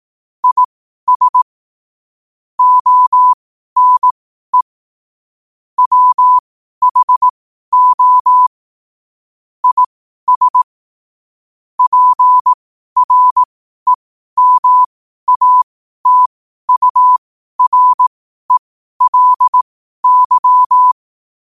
Morse code, heard as a keyed tone, about 18 wpm, spaced as for 12 wpm, 1000 Hz. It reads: IS ONE WHO IS PREMATURELY
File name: qotdfiles / 12wpm